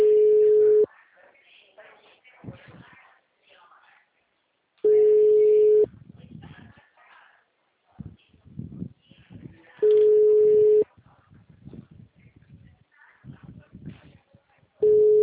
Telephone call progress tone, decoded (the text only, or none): ringback tone